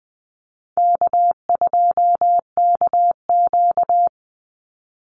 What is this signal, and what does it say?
Morse code, heard as a keyed tone, about 20 wpm, 700 Hz. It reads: K2KQ
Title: Easy Call Signs Repeated 2x - 20wpm